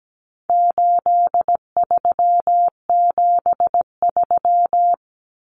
Morse code, heard as a keyed tone, about 17 wpm, 700 Hz. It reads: 8373
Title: Numbers - 4-Digit 17wpm